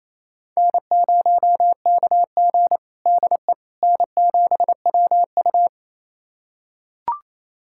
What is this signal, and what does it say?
Morse code, heard as a keyed tone, about 28 wpm, 700 Hz. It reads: N0KG DE N7WU
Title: USA Callsign DE Callsign - 4 Characters - 28wpm